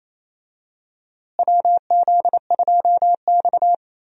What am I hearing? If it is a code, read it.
Morse code, heard as a keyed tone, about 28 wpm, 700 Hz. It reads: WZ2X
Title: USA Call Signs - 4 Characters - Repeated 2x - 28wpm